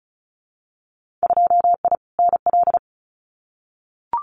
Morse code, heard as a keyed tone, about 34 wpm, 700 Hz. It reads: 2I NL
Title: Winter Field Day Exchanges Encore 35wpm